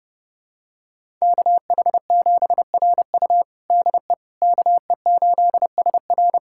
Morse code, heard as a keyed tone, about 30 wpm, 700 Hz. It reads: KH7RU DE KE8SR
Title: USA Callsign DE Callsign - 5 Characters - 30wpm